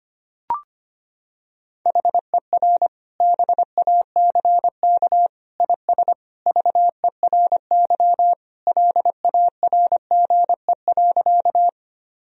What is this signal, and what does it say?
Morse code, heard as a keyed tone, about 25 wpm, 700 Hz. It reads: HER BACK IS VERY LARGE.
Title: Sentences from Top 1000 Words - Review Part 1 - 25wpm